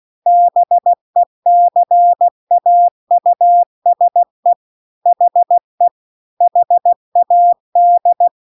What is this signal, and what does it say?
Morse code, heard as a keyed tone, about 16 wpm, 700 Hz. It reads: BECAUSE HE HAD